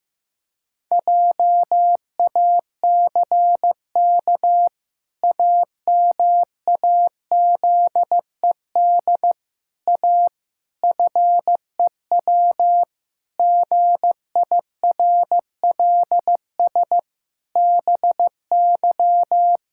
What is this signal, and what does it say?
Morse code, heard as a keyed tone, about 15 wpm, 700 Hz. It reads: JACK AMAZED A FEW GIRLS BY
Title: Pangram Challenge v2 - 15wpm